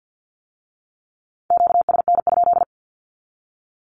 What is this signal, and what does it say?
Morse code, heard as a keyed tone, about 50 wpm, 700 Hz. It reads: QSN?
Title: Q-Codes with Spoken Meaning - 50wpm